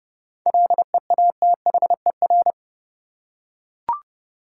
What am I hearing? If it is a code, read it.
Morse code, heard as a keyed tone, about 32 wpm, 700 Hz. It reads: LEATHER
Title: Seven Letter Words - 30wpm